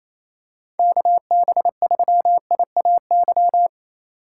Morse code, heard as a keyed tone, about 28 wpm, 700 Hz. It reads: KB3IAY
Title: USA Call Signs - 6 Characters - 3x Character Spacing - 28wpm